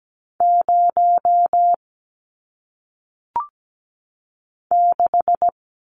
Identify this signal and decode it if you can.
Morse code, heard as a keyed tone, about 17 wpm, 700 Hz. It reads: 0 6
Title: Numbers - 1-Digit 17wpm